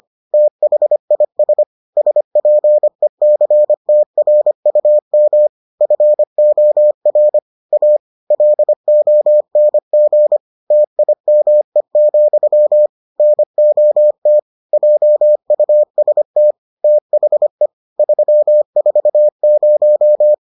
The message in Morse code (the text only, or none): THIS SPECTRUM FOR A LONG TIME, NOT JUST THE 340